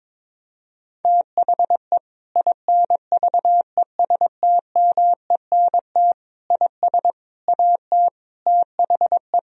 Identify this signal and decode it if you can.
Morse code, heard as a keyed tone, about 22 wpm, 700 Hz. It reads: THE INVESTMENT IS AT THE
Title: Sentences from Top 1000 Words - Review Part 4 - 22wpm